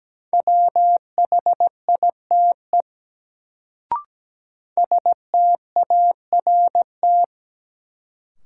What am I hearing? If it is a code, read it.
Morse code, heard as a keyed tone, about 17 wpm, 700 Hz. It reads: WHITE START
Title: Sentences from Top 300 Words - New Word Drill - 17wpm